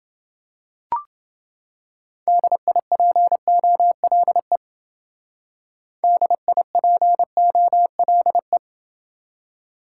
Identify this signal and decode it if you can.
Morse code, heard as a keyed tone, about 29 wpm, 700 Hz. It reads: DIPOLE DIPOLE